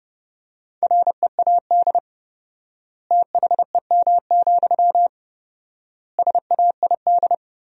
Morse code, heard as a keyed tone, about 30 wpm, 700 Hz. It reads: READ THEM, SAID
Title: 4x Word-Spacing - Sentences from Top 400 Words - 30wpm